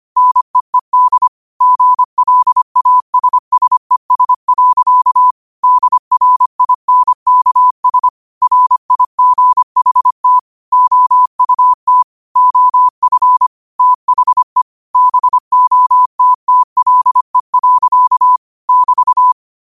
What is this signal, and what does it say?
Morse code, heard as a keyed tone, about 25 wpm, 1000 Hz. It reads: NEED GLASSES. DRINKS RIGHT OUT OF THE BOTTLE. <BT>
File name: 25wpm